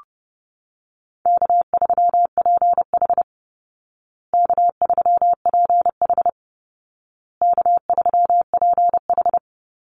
Morse code, heard as a keyed tone, about 30 wpm, 700 Hz. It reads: K3PH K3PH K3PH
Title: Call Signs Repeated 2x - 30wpm